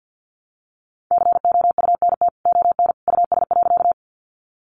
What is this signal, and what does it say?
Morse code, heard as a keyed tone, about 49 wpm, 700 Hz. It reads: COUNT ON US.
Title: Sentences from Top 1500 Words 50wpm